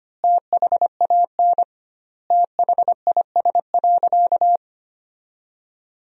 Morse code, heard as a keyed tone, about 25 wpm, 700 Hz. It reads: THAN THIS.
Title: 2x Word-Spacing - Sentences from Top 100 Words - 25wpm